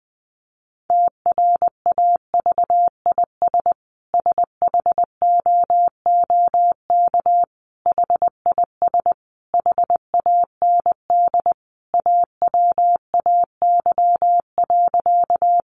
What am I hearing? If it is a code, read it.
Morse code, heard as a keyed tone, about 20 wpm, 700 Hz. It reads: TRAVIS SHOOK HIS HAND AWAY.
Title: A Sound Of Thunder 20wpm